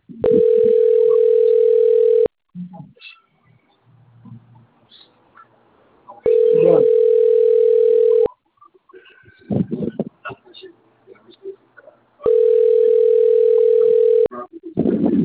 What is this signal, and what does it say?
Ringback tone (North America), 440 + 480 Hz, with speech.
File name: rg-1600-unknown- - - .69360